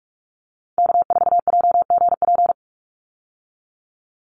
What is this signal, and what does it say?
Morse code, heard as a keyed tone, about 44 wpm, 700 Hz. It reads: K4JGP
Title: USA Call Signs 45wpm